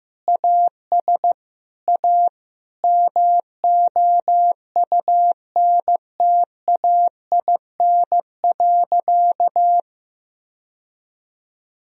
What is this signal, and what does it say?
Morse code, heard as a keyed tone, about 15 wpm, 700 Hz. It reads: AS A MOUNTAIN.
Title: Sentences from Top 1000 Words - Review Part 1 - 15wpm